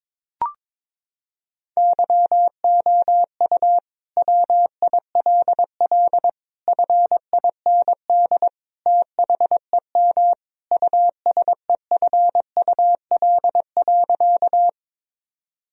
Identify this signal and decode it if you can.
Morse code, heard as a keyed tone, about 22 wpm, 700 Hz. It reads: YOU WILL FIND THEM USEFUL.
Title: Sentences from Top 1000 Words - Review Part 4 - 22wpm